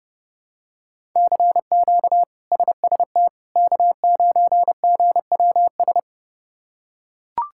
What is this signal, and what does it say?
Morse code, heard as a keyed tone, about 30 wpm, 700 Hz. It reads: CQ SST K9GWS
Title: K1USN SST QSO Practice - 30wpm